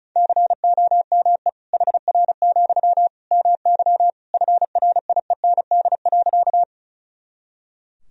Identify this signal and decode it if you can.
Morse code, heard as a keyed tone, about 34 wpm, 700 Hz. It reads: COME HR, MY FRIEND.
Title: Sentences from Top 300 Words and Abbreviations 35wpm